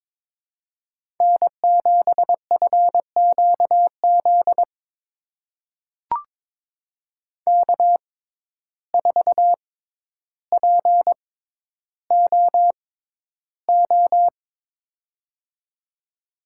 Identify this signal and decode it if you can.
Morse code, heard as a keyed tone, about 21 wpm, spaced as for 11 wpm, 700 Hz. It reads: N7FQZ K4POO
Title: USA Call Signs - 5 Characters - 6x Character Spacing - 22wpm